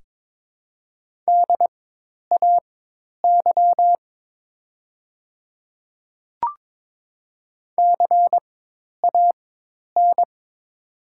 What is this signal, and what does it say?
Morse code, heard as a keyed tone, about 23 wpm, spaced as for 10 wpm, 700 Hz. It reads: DAY CAN
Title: 4x Character Spacing - Top 100 Words - 22wpm